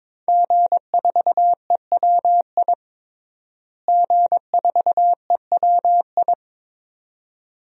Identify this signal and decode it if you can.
Morse code, heard as a keyed tone, about 22 wpm, 700 Hz. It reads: G4EWI G4EWI
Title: DXCC-100 Call Signs - 5 Characters - Repeated 2x - 22wpm